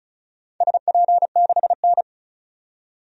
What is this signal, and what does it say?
Morse code, heard as a keyed tone, about 34 wpm, 700 Hz. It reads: SP6N
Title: DXCC-100 Call Signs - 4 Characters - 35wpm